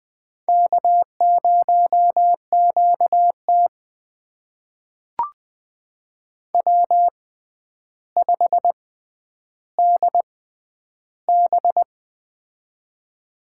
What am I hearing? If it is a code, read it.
Morse code, heard as a keyed tone, about 20 wpm, spaced as for 10 wpm, 700 Hz. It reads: K0QT W5DB
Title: USA Call Signs - 4 Characters - 6x Character Spacing - 20wpm